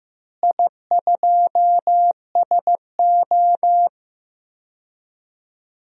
Morse code, heard as a keyed tone, about 15 wpm, 700 Hz. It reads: I2SO
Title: DXCC-100 Call Signs - 4 Characters - Repeated 3x - 15wpm